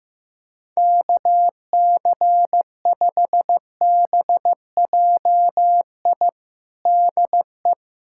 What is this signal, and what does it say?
Morse code, heard as a keyed tone, about 16 wpm, 700 Hz. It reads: KC5BJI DE
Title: USA Callsign DE Callsign - 6 Characters - 15wpm